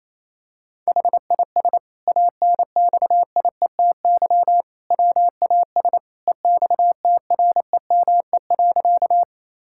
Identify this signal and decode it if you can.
Morse code, heard as a keyed tone, about 28 wpm, 700 Hz. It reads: HIS ANXIETY WAS EXTREME.
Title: Sentences from Top 1600 Words 28wpm